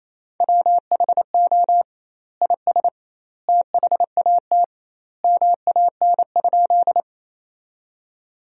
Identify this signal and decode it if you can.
Morse code, heard as a keyed tone, about 28 wpm, 700 Hz. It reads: WHO IS THAT MAN?
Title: 2x Word-Spacing - Sentences from Top 200 Words - 28wpm